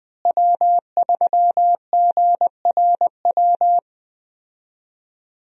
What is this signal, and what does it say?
Morse code, heard as a keyed tone, about 21 wpm, 700 Hz. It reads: W3GRW
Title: USA Call Signs - 5 Characters - Repeated 2x - 20wpm